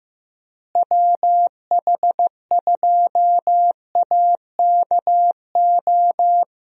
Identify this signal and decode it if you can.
Morse code, heard as a keyed tone, about 15 wpm, 700 Hz. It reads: WH2AKO
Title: USA Call Signs - 6 Characters - Repeated 2x - 15wpm